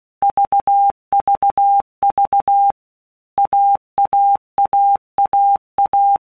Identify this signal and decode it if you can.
Morse code, heard as a keyed tone, about 16 wpm, 800 Hz. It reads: VVV AAAAA
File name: All Groups 1-minute Blocks / Groups@15 / AWUNDG